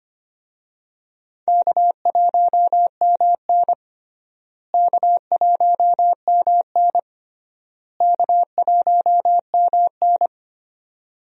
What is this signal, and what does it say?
Morse code, heard as a keyed tone, about 25 wpm, 700 Hz. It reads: K1MN K1MN K1MN
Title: USA Call Signs - 4 Characters - Repeated 3x - 25wpm